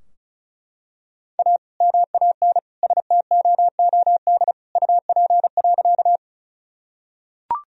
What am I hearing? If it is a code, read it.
Morse code, heard as a keyed tone, about 36 wpm, 700 Hz. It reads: A MAN STOOD UP.
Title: Sentences from Top 1000 Words - Review Part 2 - 35wpm